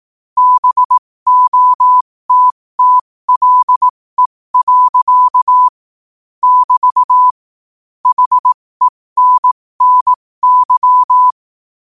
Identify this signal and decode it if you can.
Morse code, heard as a keyed tone, about 18 wpm, spaced as for 15 wpm, 1000 Hz. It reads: BOTTLE. <BT> HENNY
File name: qotdfiles / 15wpm